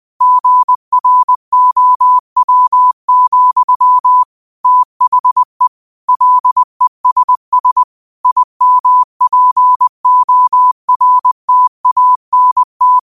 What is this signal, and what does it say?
Morse code, heard as a keyed tone, about 20 wpm, 1000 Hz. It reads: GROW, THE LESS IMPORTANT